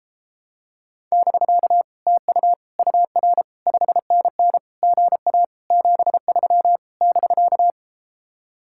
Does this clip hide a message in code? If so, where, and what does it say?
Yes: Morse code from 1.1 to 7.7 s, heard as a keyed tone, about 33 wpm, 700 Hz. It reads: <BK> TU UR 5NN GA 73 <BK>